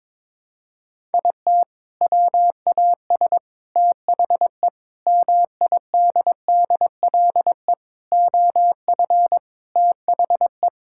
Morse code, heard as a keyed tone, about 22 wpm, 700 Hz. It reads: IT WAS THE MIDDLE OF THE
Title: Sentences from Top 1000 Words - Review Part 6 - 22wpm